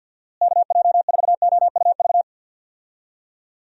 Morse code, heard as a keyed tone, about 50 wpm, 700 Hz. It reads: KJ4OUV
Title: USA Call Signs - 6 Characters - Repeated 3x - 50wpm